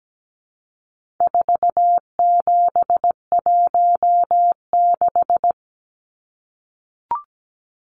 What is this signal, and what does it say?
Morse code, heard as a keyed tone, about 17 wpm, 700 Hz. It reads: 4716
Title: Numbers - 4-Digit 17wpm